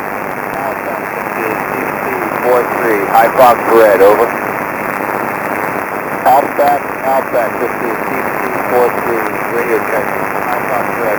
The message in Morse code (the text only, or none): none